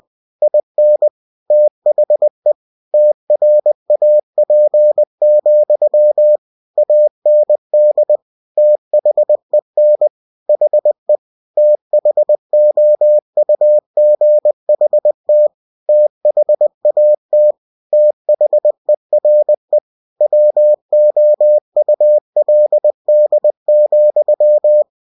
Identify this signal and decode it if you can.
Morse code, heard as a keyed tone, about 20 wpm, 600 Hz. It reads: IN THE TRAP, AND THEN HE THOUGHT THAT THERE WOULD,
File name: cw-books / winnie-the-pooh / winnie-20wpm-page-0041